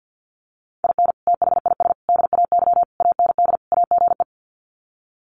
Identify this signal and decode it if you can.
Morse code, heard as a keyed tone, about 50 wpm, 700 Hz. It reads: IN THIS DAY AND AGE
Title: Sets of 5 Words Encore 50wpm